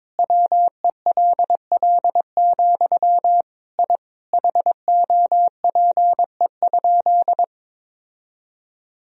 Morse code, heard as a keyed tone, about 22 wpm, 700 Hz. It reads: WELL, I HOPE?
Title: Sentences from Top 1000 Words - Review Part 3 - 22wpm